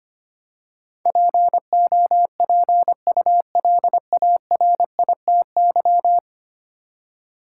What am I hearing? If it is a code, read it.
Morse code, heard as a keyed tone, about 25 wpm, 700 Hz. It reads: POPULARITY
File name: 10 Letter Words - First 3 Letters Given - 25wpm